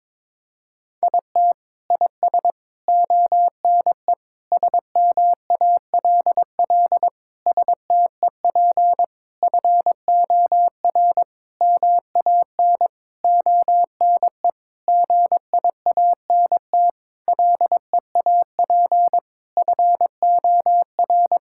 Morse code, heard as a keyed tone, about 22 wpm, 700 Hz. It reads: IT IS ONE SMALL STEP FOR MAN ONE GIANT LEAP FOR
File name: Sentences with 9-Word Phrases - 22wpm